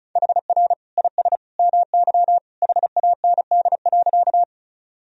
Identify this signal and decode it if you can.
Morse code, heard as a keyed tone, about 34 wpm, 700 Hz. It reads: HR IS MY HAND.
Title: Sentences from Top 200 Words and Abbreviations 35wpm